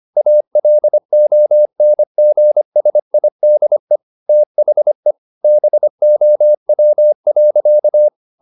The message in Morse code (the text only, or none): ALONGSIDE THE BOW.